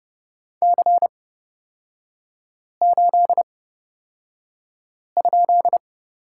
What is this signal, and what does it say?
Morse code, heard as a keyed tone, about 30 wpm, spaced as for 5 wpm, 700 Hz. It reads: C8?
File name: Lesson 36e - BT - Three Characters 30wpm